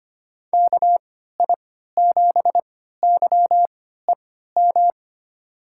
Morse code, heard as a keyed tone, about 25 wpm, spaced as for 14 wpm, 700 Hz. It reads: KI7YEM
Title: USA Call Signs - 6 Characters - 3x Character Spacing - 25wpm